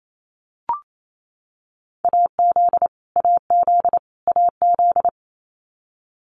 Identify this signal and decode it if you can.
Morse code, heard as a keyed tone, about 28 wpm, 700 Hz. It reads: AZ AZ AZ
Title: US States Repeated 3x 28wpm